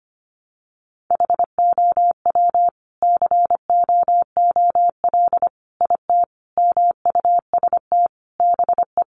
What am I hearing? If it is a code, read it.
Morse code, heard as a keyed tone, about 25 wpm, 700 Hz. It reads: HOW COOL IT MUST BE
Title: Sentences from Top 1200 Words 25wpm